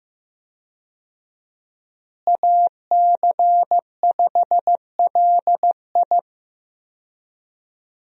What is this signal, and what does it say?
Morse code, heard as a keyed tone, about 15 wpm, 700 Hz. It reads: AC5LI
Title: USA Call Signs - 5 Characters - Repeated 2x - 15wpm